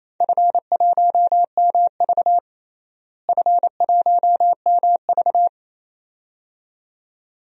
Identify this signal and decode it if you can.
Morse code, heard as a keyed tone, about 27 wpm, 700 Hz. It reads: F1MV F1MV
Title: DXCC-100 Call Signs - 4 Characters - Repeated 2x - 28wpm